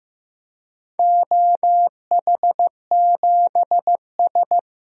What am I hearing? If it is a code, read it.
Morse code, heard as a keyed tone, about 15 wpm, 700 Hz. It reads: OH7S
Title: DXCC-100 Call Signs - 4 Characters - 15wpm